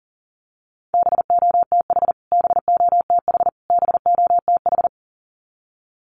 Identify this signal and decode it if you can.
Morse code, heard as a keyed tone, about 40 wpm, 700 Hz. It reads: BOTH BOTH BOTH
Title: Top 500 Words Repeated 40wpm